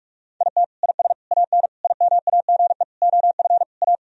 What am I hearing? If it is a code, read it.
Morse code, heard as a keyed tone, about 41 wpm, 700 Hz. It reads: IT IS AN IMAGE OF A
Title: Sentences from Top 1000 Words - Review Part 4 - 45wpm